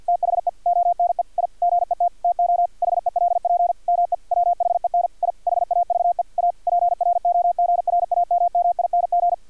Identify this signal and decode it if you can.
Morse code, heard as a keyed tone, about 50 wpm, 700 Hz. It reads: THE ONE I GET TO HELP ME WHEN I HAVE A PROGRAMMING
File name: archived / CRSnet